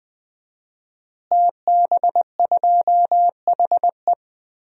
Morse code, heard as a keyed tone, about 21 wpm, 700 Hz. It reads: TB2HE
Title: DXCC-100 Call Signs - 5 Characters - Repeated 3x - 20wpm